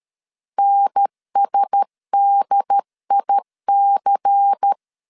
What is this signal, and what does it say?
Morse code, heard as a keyed tone, about 13 wpm, 800 Hz. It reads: NSDIC
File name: CW / DET / 1 6 1500C S C 12wpm 3533